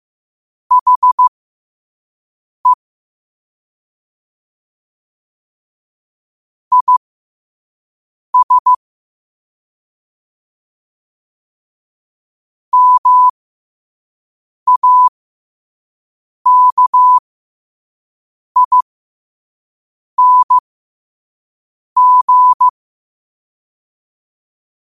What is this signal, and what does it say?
Morse code, heard as a keyed tone, about 15 wpm, spaced as for 5 wpm, 1000 Hz. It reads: HE IS MAKING
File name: qotdfiles / 5wpm